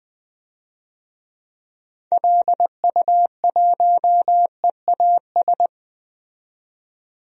Morse code, heard as a keyed tone, about 21 wpm, 700 Hz. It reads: LU1EAS
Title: DXCC-100 Call Signs - 6 Characters - 20wpm